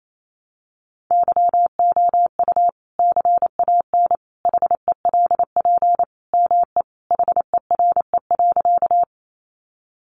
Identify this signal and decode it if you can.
Morse code, heard as a keyed tone, about 28 wpm, 700 Hz. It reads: YOU CAN HELP ME HERE.
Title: Sentences from Top 200 Words 28wpm